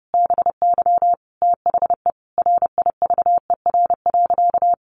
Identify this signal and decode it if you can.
Morse code, heard as a keyed tone, about 30 wpm, 700 Hz. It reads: BY THE RIVER.
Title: Sentences from Top 1200 Words 30wpm